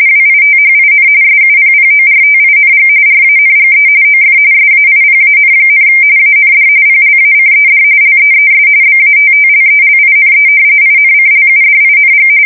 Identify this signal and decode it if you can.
RTTY, about 45.45 baud. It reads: WITHIN SUPPORTING RANGE OF SAIPAN, AND PLACED LEES BATTLELINE INCLUDING
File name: fsk170 / battleshipnj